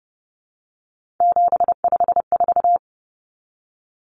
Morse code, heard as a keyed tone, about 28 wpm, 700 Hz. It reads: 754
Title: Common Cut Numbers - 3-Digit 30wpm